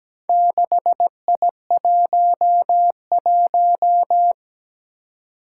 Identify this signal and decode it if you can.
Morse code, heard as a keyed tone, about 17 wpm, 700 Hz. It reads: 6I11
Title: Four Challenging Letter-Number Combinations 17wpm